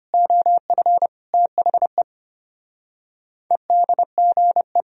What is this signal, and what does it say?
Morse code, heard as a keyed tone, about 31 wpm, 700 Hz. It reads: OF THE EDGE
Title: Speed-Racing - Sets of 3 Words 25wpm